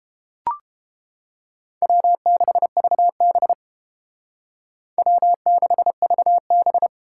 Morse code, heard as a keyed tone, about 32 wpm, 700 Hz. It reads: W6VB W6VB